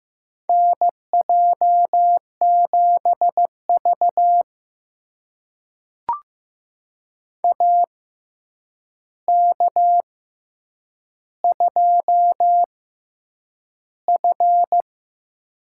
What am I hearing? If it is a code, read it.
Morse code, heard as a keyed tone, about 15 wpm, spaced as for 8 wpm, 700 Hz. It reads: NJ7V AK2F
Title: USA Call Signs - 4 Characters - 6x Character Spacing - 15wpm